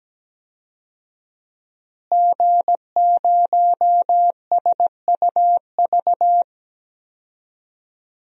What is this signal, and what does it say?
Morse code, heard as a keyed tone, about 17 wpm, 700 Hz. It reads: G0SUV
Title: DXCC-100 Call Signs - 5 Characters - Repeated 2x - 17wpm